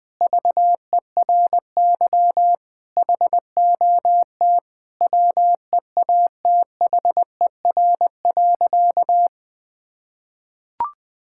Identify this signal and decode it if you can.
Morse code, heard as a keyed tone, about 20 wpm, 700 Hz. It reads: VERY HOT WEATHER.
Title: Sentences from Top 1000 Words - Review Part 5 - 20wpm